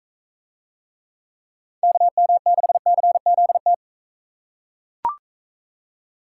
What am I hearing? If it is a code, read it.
Morse code, heard as a keyed tone, about 42 wpm, 700 Hz. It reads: KM6CZT